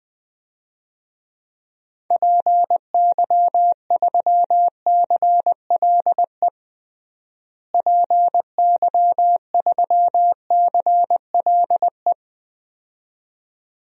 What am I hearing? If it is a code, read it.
Morse code, heard as a keyed tone, about 20 wpm, 700 Hz. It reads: PY3CLE PY3CLE
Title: DXCC-100 Call Signs - 6 Characters - Repeated 2x - 20wpm